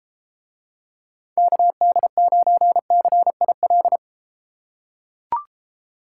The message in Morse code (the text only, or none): KD9CIL